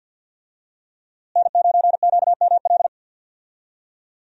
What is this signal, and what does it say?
Morse code, heard as a keyed tone, about 52 wpm, 700 Hz. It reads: N9QML
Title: USA Call Signs - 5 Characters - Repeated 2x - 50wpm